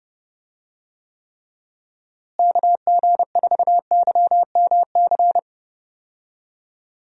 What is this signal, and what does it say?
Morse code, heard as a keyed tone, about 30 wpm, 700 Hz. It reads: KG4YMC
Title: USA Call Signs - 6 Characters - Repeated 2x - 30wpm